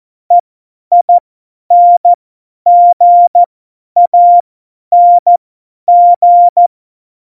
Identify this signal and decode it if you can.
Morse code, heard as a keyed tone, about 14 wpm, spaced as for 10 wpm, 700 Hz. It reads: EINGANG
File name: Morsekurs / Witze / Witz 01. September 2020, 50 F70